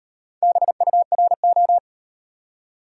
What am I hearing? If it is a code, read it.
Morse code, heard as a keyed tone, about 34 wpm, 700 Hz. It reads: BURO